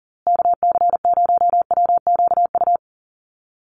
Morse code, heard as a keyed tone, about 40 wpm, 700 Hz. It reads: KC0WQU
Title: USA Call Signs Repeated 2x 40wpm